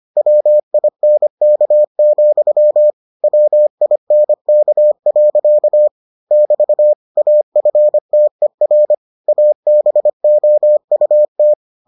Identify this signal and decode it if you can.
Morse code, heard as a keyed tone, about 25 wpm, 600 Hz. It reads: WINK, WINK. <BT> AFTER ABOUT